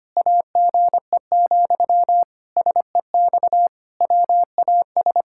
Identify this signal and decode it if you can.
Morse code, heard as a keyed tone, about 25 wpm, 700 Hz. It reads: AGE, SEX WAS
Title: Sentences from Top 1600 Words 25wpm